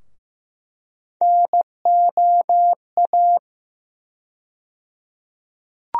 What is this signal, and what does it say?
Morse code, heard as a keyed tone, about 15 wpm, 700 Hz. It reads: NOA
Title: USA Call Sign Postfixes - 3 Letters 15wpm